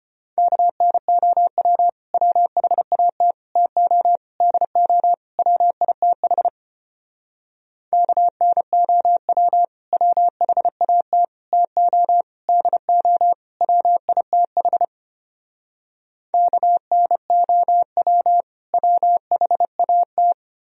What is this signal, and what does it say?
Morse code, heard as a keyed tone, about 33 wpm, 700 Hz. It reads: KNOW WHAT TO DO WITH KNOW WHAT TO DO WITH KNOW WHAT